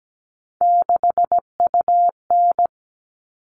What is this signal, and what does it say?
Morse code, heard as a keyed tone, about 17 wpm, 700 Hz. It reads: 6UN
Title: All Cut Numbers - 3-Digit 17wpm